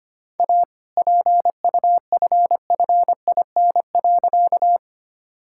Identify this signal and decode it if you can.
Morse code, heard as a keyed tone, about 25 wpm, 700 Hz. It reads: A PUFFIN.
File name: practice / 25wpm / All-Age Appropriate Jokes - 25wpm